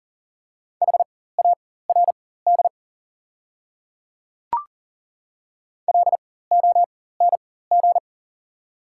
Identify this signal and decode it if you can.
Morse code, heard as a keyed tone, about 40 wpm, spaced as for 19 wpm, 700 Hz. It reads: HARD LONG
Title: 4x Character Spacing - Top 300 Words - 40wpm